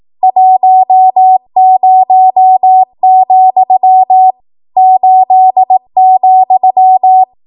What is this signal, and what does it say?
Morse code, heard as a keyed tone, about 18 wpm, 750 Hz. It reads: 10, 8,
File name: Morse / 18WPM